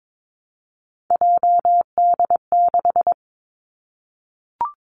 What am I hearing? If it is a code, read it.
Morse code, heard as a keyed tone, about 22 wpm, 700 Hz. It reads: JD6
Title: Three Challenging Letter-Number Combinations 22wpm